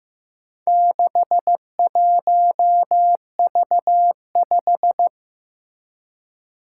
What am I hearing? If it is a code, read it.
Morse code, heard as a keyed tone, about 15 wpm, 700 Hz. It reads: 61V5
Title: Four Challenging Letter-Number Combinations 15wpm